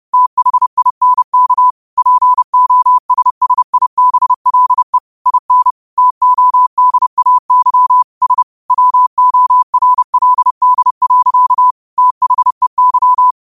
Morse code, heard as a keyed tone, about 30 wpm, 1000 Hz. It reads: THINK POSSIBLE IN TODAY S WORLD. THEY